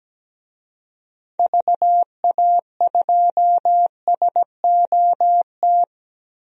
Morse code, heard as a keyed tone, about 17 wpm, 700 Hz. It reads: VA2SOT
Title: DXCC-100 Call Signs - 6 Characters - 17wpm